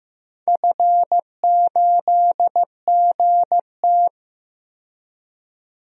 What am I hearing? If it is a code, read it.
Morse code, heard as a keyed tone, about 15 wpm, 700 Hz. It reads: F8GT
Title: DXCC-100 Call Signs - 4 Characters - 15wpm